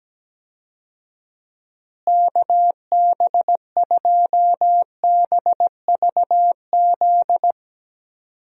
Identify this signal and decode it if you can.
Morse code, heard as a keyed tone, about 17 wpm, 700 Hz. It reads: KB2BVZ
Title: USA Call Signs - 6 Characters - Repeated 2x - 17wpm